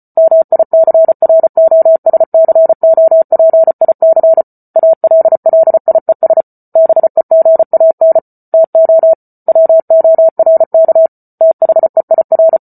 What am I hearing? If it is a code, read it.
Morse code, heard as a keyed tone, about 34 wpm, 650 Hz. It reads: MICROSCOPIC ALLIES BEGAN TO WORK THEIR
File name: Book Two / 08-Chapter 8 - 34 WPM